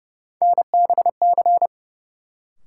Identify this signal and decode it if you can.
Morse code, heard as a keyed tone, about 31 wpm, 700 Hz. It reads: NBC
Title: Sets of 5 Words Encore - New Word Drill - 30wpm